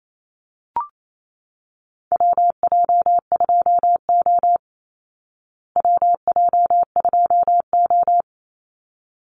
Morse code, heard as a keyed tone, about 28 wpm, 700 Hz. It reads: WJ2O WJ2O
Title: Call Signs Repeated 3x - 28wpm